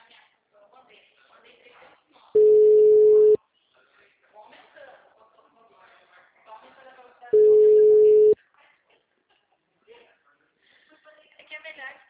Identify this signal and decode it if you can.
Ringback tone (Europe, 425 Hz), with speech.
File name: out- -2005- - - .75795